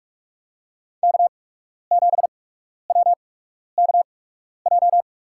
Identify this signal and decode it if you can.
Morse code, heard as a keyed tone, about 46 wpm, spaced as for 12 wpm, 700 Hz. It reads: K7WKJ
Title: USA Call Signs - 5 Characters - 8x Character Spacing - 45wpm